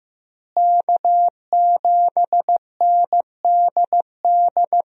Morse code, heard as a keyed tone, about 15 wpm, 700 Hz. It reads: K7NDD
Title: USA Call Signs - 5 Characters - Repeated 3x - 15wpm